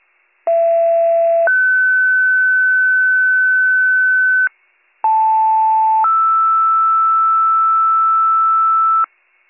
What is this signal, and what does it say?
Two-tone page, heard as a one-second tone, then a three-second tone, about 670 then 1500 Hz, about 880 then 1320 Hz, with noise. It is heard from 0.5 to 9.1 s.